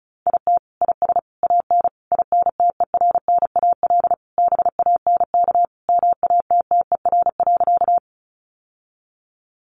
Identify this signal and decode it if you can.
Morse code, heard as a keyed tone, about 34 wpm, 700 Hz. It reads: IT IS AN INTERNAL BANK MATTER.
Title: Sentences from Top 1400 Words 35wpm